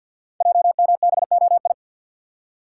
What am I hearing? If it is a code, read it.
Morse code, heard as a keyed tone, about 49 wpm, 700 Hz. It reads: JMBOI
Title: Code Groups - 5 Letters - 50wpm